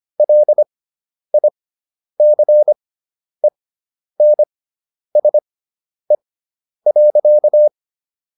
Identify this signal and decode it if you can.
Morse code, heard as a keyed tone, about 25 wpm, spaced as for 10 wpm, 600 Hz. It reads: LICENSE.